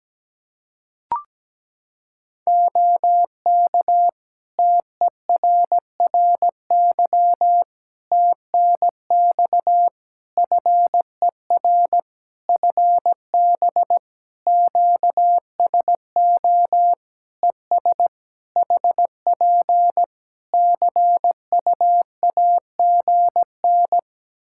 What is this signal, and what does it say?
Morse code, heard as a keyed tone, about 17 wpm, 700 Hz. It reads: OK TERRY TNX FER FB QSO ES HP CUAGN
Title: Practice QSO - Exchange 4 - Ending - 17wpm